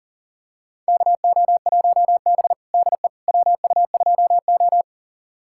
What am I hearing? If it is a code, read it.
Morse code, heard as a keyed tone, about 40 wpm, 690 Hz. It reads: KO1B DE WU2O
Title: USA Callsign DE Callsign - 4 Characters - 40wpm